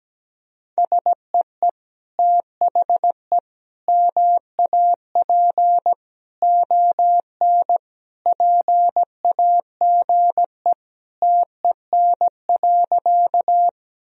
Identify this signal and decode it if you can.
Morse code, heard as a keyed tone, about 17 wpm, 700 Hz. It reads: SEE THE MAP ON PAGE TEN.
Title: Sentences from Top 1000 Words - Review Part 2 - 17wpm